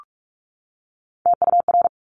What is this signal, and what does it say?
Morse code, heard as a keyed tone, about 46 wpm, 700 Hz. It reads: TUR
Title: Top Three Letter Combinations in Words 45wpm